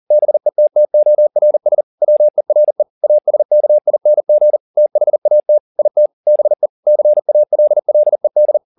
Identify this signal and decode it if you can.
Morse code, heard as a keyed tone, about 40 wpm, 600 Hz. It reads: BETTORS WERE ASKING THAT IT BE CALLED
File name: the old man and the sea 2 40wpm